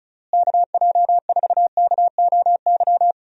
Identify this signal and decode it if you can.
Morse code, heard as a keyed tone, about 34 wpm, 700 Hz. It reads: KJ4KOY
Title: USA Call Signs - 6 Characters - Repeated 3x - 35wpm